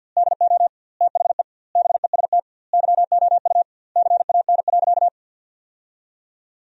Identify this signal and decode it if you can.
Morse code, heard as a keyed tone, about 51 wpm, 700 Hz. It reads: DO THE BEST YOU CAN.
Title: 2x Word-Spacing - Sentences from Top 500 Words - 50wpm